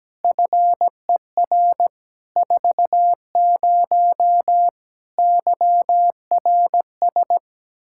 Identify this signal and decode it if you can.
Morse code, heard as a keyed tone, about 17 wpm, 700 Hz. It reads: FER 40 YRS
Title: Practice QSO - Exchange 3 - More Info - 17wpm